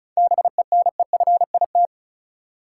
Morse code, heard as a keyed tone, about 36 wpm, 700 Hz. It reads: BENEFIT
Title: Sentences from Top 1000 Words - Review Part 6 - New Word Drill - 35wpm